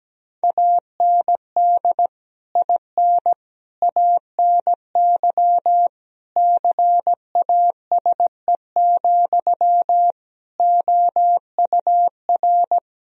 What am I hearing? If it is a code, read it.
Morse code, heard as a keyed tone, about 17 wpm, 700 Hz. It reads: AND IN ANY CASE, OUR
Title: Sentences from Top 1000 Words - Review Part 1 - 17wpm